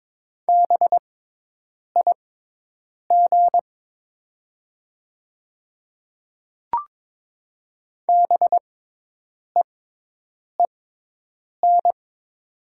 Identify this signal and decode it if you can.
Morse code, heard as a keyed tone, about 22 wpm, spaced as for 8 wpm, 700 Hz. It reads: BIG BEEN
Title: 6x Character Spacing - Top 500 Words - 22wpm